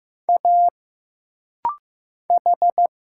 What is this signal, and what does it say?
Morse code, heard as a keyed tone, about 15 wpm, 700 Hz. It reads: A H
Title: Single Letters - Rapid-Fire 15wpm